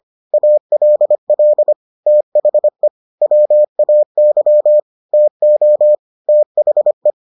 Morse code, heard as a keyed tone, about 25 wpm, 600 Hz. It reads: ALL THE WAY TO THE